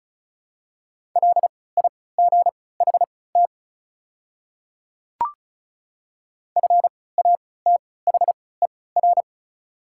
Morse code, heard as a keyed tone, about 36 wpm, spaced as for 20 wpm, 690 Hz. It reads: LIGHT FATHER
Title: 3x Character Spacing - Top 200 Words - 35wpm